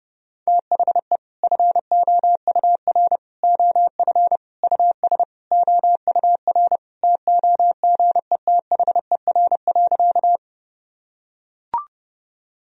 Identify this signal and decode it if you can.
Morse code, heard as a keyed tone, about 30 wpm, 700 Hz. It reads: THE FOUR OF US OUR TOGETHER.
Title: Sentences from Top 1000 Words - Review Part 1 - 30wpm